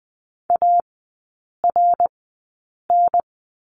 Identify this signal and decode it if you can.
Morse code, heard as a keyed tone, about 20 wpm, spaced as for 8 wpm, 690 Hz. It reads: ARN
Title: 093 Lesson 16e - Letter c - Three Characters 20wpm